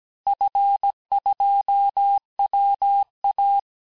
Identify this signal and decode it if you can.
Morse code, heard as a keyed tone, about 17 wpm, 800 Hz. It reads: F2WA
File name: corrige-cours-1581-fin-du-QSO-vitesse-16-mpm